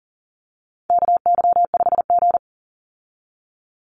Morse code, heard as a keyed tone, about 40 wpm, 700 Hz. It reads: KY5G
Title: USA Call Signs 40wpm